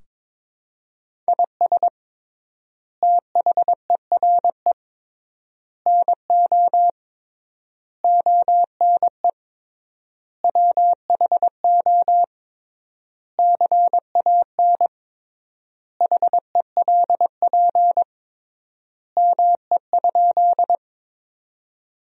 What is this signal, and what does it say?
Morse code, heard as a keyed tone, about 22 wpm, 700 Hz. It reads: IS THERE NO ONE WHO CAN HELP ME?
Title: 3x Word-Spacing - Sentences from Top 200 Words - 22wpm